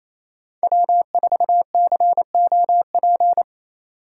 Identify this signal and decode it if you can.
Morse code, heard as a keyed tone, about 28 wpm, 700 Hz. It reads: W4COP
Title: USA Call Signs - 8x Character Spacing - 28wpm